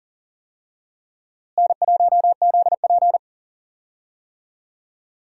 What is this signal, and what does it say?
Morse code, heard as a keyed tone, about 40 wpm, 700 Hz. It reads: N1ZP
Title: USA Call Signs - 4 Characters - Repeated 3x - 40wpm